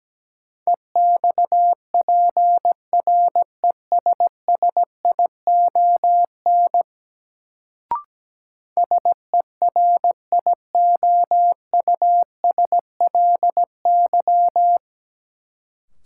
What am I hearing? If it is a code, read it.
Morse code, heard as a keyed tone, about 17 wpm, 700 Hz. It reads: EXPRESSION SERIOUSLY
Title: Sentences from Top 1200 Words - New Word Drill Alt - 17wpm